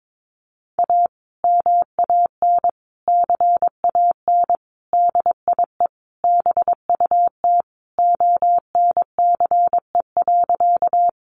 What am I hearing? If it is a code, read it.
Morse code, heard as a keyed tone, about 22 wpm, 700 Hz. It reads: A MAN CAN DIE BUT ONCE.
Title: Sentences from Top 1300 Words 22wpm